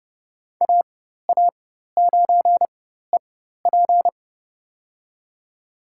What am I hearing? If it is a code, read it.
Morse code, heard as a keyed tone, about 30 wpm, spaced as for 14 wpm, 700 Hz. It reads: AA9EP
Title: USA Call Signs - 5 Characters - 4x Character Spacing - 30wpm